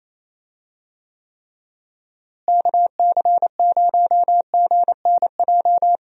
Morse code, heard as a keyed tone, about 28 wpm, 700 Hz. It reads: KC0GNJ
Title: USA Call Signs - 6 Characters - Repeated 3x - 28wpm